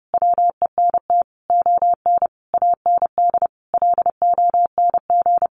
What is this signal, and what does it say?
Morse code, heard as a keyed tone, about 30 wpm, 700 Hz. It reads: WENT ON AND LONG
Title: Sentences from Top 1500 Words 30wpm